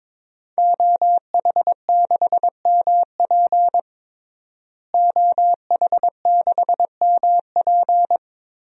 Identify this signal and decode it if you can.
Morse code, heard as a keyed tone, about 22 wpm, 700 Hz. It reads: OH6MP OH6MP
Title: DXCC-100 Call Signs - 5 Characters - Repeated 3x - 22wpm